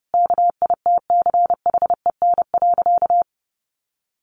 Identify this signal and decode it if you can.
Morse code, heard as a keyed tone, about 30 wpm, 700 Hz. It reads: KITCHEN.
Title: Sentences from Top 1400 Words 30wpm